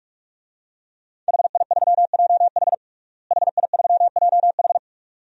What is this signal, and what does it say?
Morse code, heard as a keyed tone, about 46 wpm, 700 Hz. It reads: HI3JH HI3JH
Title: DXCC-100 Call Signs - 5 Characters - Repeated 2x - 45wpm